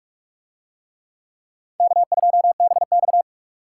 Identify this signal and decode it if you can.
Morse code, heard as a keyed tone, about 40 wpm, 700 Hz. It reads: K2BX